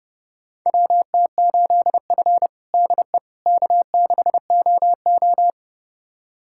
Morse code, heard as a keyed tone, about 31 wpm, 700 Hz. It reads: WT8F DE K6OO
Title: USA Callsign DE Callsign - 4 Characters - 30wpm